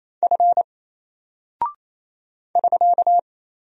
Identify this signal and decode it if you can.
Morse code, heard as a keyed tone, about 27 wpm, 700 Hz. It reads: F <SK>
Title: Single Character (Letter:Number:Punctuation:Prosign) 28wpm